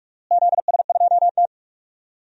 Morse code, heard as a keyed tone, about 43 wpm, 700 Hz. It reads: ZS2T
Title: DXCC-100 Call Signs - 4 Characters - Repeated 3x - 45wpm